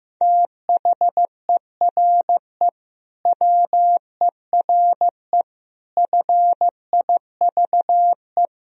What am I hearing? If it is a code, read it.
Morse code, heard as a keyed tone, about 15 wpm, 700 Hz. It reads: THERE WERE FIVE
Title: Sentences from Top 1000 Words - Review Part 3 - 15wpm